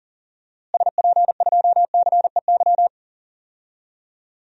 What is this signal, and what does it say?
Morse code, heard as a keyed tone, about 40 wpm, 700 Hz. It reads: SP2CEY
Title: DXCC-100 Call Signs - 6 Characters - Repeated 3x - 40wpm